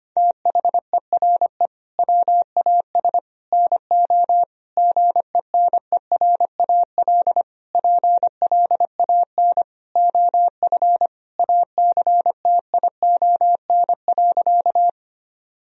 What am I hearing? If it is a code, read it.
Morse code, heard as a keyed tone, about 25 wpm, 700 Hz. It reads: THERE WAS NO GENERAL PLAN OF ACTION.
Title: Sentences from Top 1000 Words - Review Part 4 - 25wpm